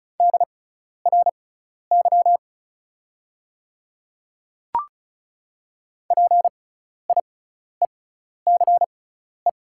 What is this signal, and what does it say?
Morse code, heard as a keyed tone, about 34 wpm, spaced as for 12 wpm, 690 Hz. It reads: DRY PIECE
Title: 6x Character Spacing - Top 500 Words - 35wpm